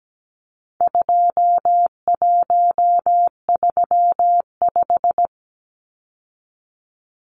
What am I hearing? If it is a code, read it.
Morse code, heard as a keyed tone, about 17 wpm, 700 Hz. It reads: 2135
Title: Numbers - 4-Digit 17wpm